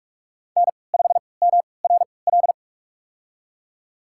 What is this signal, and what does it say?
Morse code, heard as a keyed tone, about 42 wpm, spaced as for 25 wpm, 700 Hz. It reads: N5MRL
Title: USA Call Signs - 5 Characters - 3x Character Spacing - 45wpm